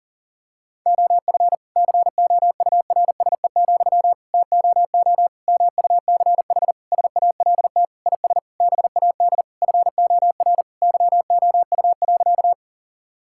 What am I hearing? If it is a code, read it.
Morse code, heard as a keyed tone, about 40 wpm, 700 Hz. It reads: OF COURSE, TOO MUCH SALT IS BAD FOR YOU.
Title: Sentences from Top 1000 Words - Review Part 5 - 40wpm